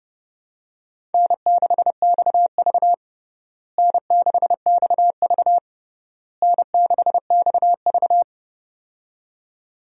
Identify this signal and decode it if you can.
Morse code, heard as a keyed tone, about 30 wpm, 700 Hz. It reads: N6XV N6XV N6XV
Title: USA Call Signs - 4 Characters - Repeated 3x - 30wpm